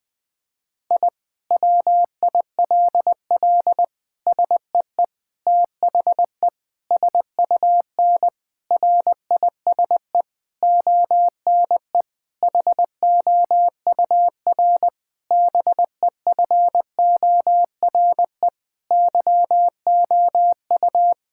Morse code, heard as a keyed tone, about 19 wpm, 700 Hz. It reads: I WILL SEE THE SUN RISE ONE HOUR BEFORE YOU
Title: Sentences from Top 1000 Words - Review Part 6 - 20wpm